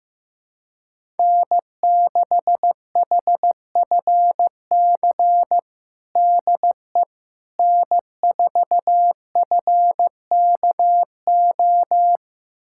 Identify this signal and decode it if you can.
Morse code, heard as a keyed tone, about 15 wpm, 700 Hz. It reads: N6HFC DE N4FKO
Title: USA Callsign DE Callsign - 5 Characters - 15wpm